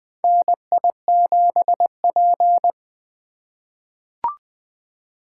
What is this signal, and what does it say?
Morse code, heard as a keyed tone, about 20 wpm, 700 Hz. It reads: NI7P
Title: USA Call Signs - 4 Characters - 3x Character Spacing - 20wpm